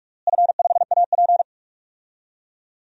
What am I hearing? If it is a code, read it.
Morse code, heard as a keyed tone, about 46 wpm, 700 Hz. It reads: F5AP
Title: DXCC-100 Call Signs - 4 Characters - Repeated 3x - 45wpm